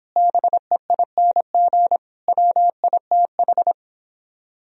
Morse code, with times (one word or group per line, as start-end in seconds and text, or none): 0.2-2.0 BEING
2.3-3.7 WITH